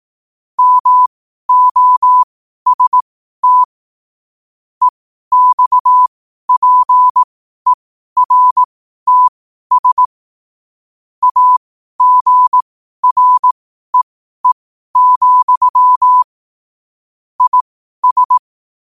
Morse code, heard as a keyed tone, about 18 wpm, spaced as for 12 wpm, 1000 Hz. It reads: MOST EXPERTS AGREE, IS
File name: qotdfiles / 12wpm